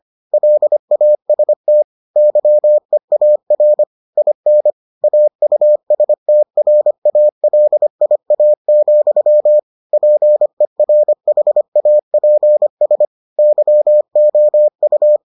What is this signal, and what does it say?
Morse code, heard as a keyed tone, about 25 wpm, 600 Hz. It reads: LAST YEAR IN AUSTRALIA, PERHAPS YOU